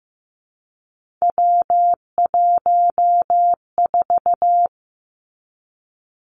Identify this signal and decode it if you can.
Morse code, heard as a keyed tone, about 15 wpm, 700 Hz. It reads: W14
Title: Three Challenging Letter-Number Combinations 15wpm